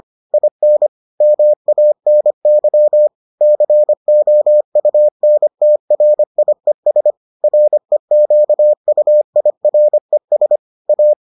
Morse code, heard as a keyed tone, about 25 wpm, 600 Hz. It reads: IN MANY COUNTRIES REQUIRES A